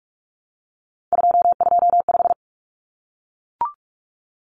Morse code, heard as a keyed tone, about 44 wpm, 700 Hz. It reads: 225
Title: Common Cut Numbers - 3-Digit 45wpm